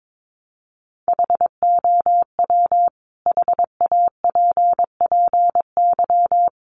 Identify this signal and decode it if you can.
Morse code, heard as a keyed tone, about 22 wpm, 700 Hz. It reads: HOW HAPPY
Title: Sentences from Top 1000 Words 22wpm